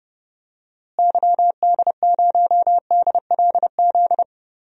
Morse code, heard as a keyed tone, about 30 wpm, 700 Hz. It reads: YD0DLZ
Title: DXCC-100 Call Signs - 6 Characters - Repeated 3x - 30wpm